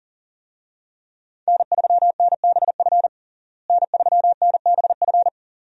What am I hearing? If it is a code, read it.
Morse code, heard as a keyed tone, about 40 wpm, 700 Hz. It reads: N3NBF N3NBF
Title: USA Call Signs - 5 Characters - Repeated 2x - 40wpm